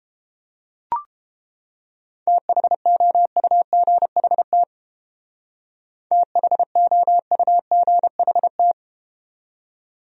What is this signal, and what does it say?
Morse code, heard as a keyed tone, about 30 wpm, 700 Hz. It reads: THOUGHT THOUGHT